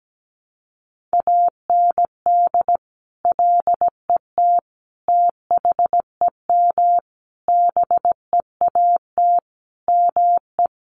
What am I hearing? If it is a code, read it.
Morse code, heard as a keyed tone, about 17 wpm, 700 Hz. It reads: AND LET THEM BEAT ME
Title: Sentences from Top 1300 Words 17wpm